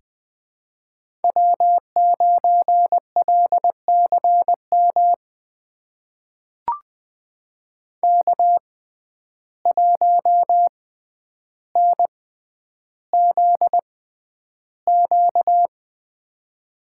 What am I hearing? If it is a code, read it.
Morse code, heard as a keyed tone, about 20 wpm, spaced as for 10 wpm, 700 Hz. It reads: W9LCM K1NZQ
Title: USA Call Signs - 5 Characters - 6x Character Spacing - 20wpm